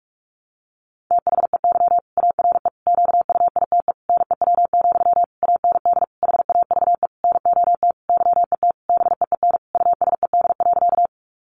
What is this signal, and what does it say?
Morse code, heard as a keyed tone, about 44 wpm, 700 Hz. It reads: THEY ARE QUITE NEW, AND HAVE NOT YET BEEN USED.
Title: Sentences from Top 700 Words 45wpm